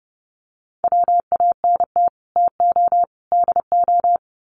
Morse code, heard as a keyed tone, about 30 wpm, 700 Hz. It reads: WANT TO DO
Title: Sets of 3 Words 30wpm